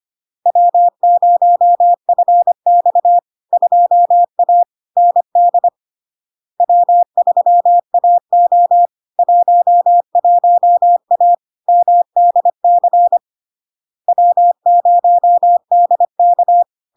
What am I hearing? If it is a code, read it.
Morse code, heard as a keyed tone, about 25 wpm, 700 Hz. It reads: W0FX 2A ND W3AO 11A MDC W0DK
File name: LARK-FD-2022-Log-in-CW-25wpm-700-Hz